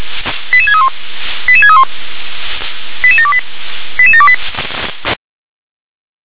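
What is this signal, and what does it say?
Five-tone selective call (ZVEI), heard as quick runs of five tones, tones of about 70 ms. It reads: 88631 88631 88728 88728
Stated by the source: Five-tone ZVEI selective call received on SDR.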